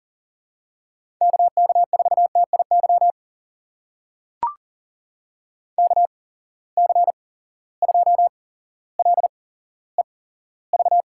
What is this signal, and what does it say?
Morse code, heard as a keyed tone, about 40 wpm, spaced as for 17 wpm, 700 Hz. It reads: KK4TIY KC2LEV
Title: USA Call Signs - 6 Characters - 8x Character Spacing - 40wpm